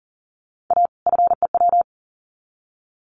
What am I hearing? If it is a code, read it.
Morse code, heard as a keyed tone, about 40 wpm, 700 Hz. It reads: A FEW
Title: Sets of 2 Words 40wpm